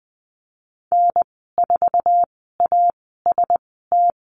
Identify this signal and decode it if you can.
Morse code, heard as a keyed tone, about 20 wpm, spaced as for 14 wpm, 700 Hz. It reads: N4AST
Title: 055 Lesson 10g - Letter r - Call Signs 20wpm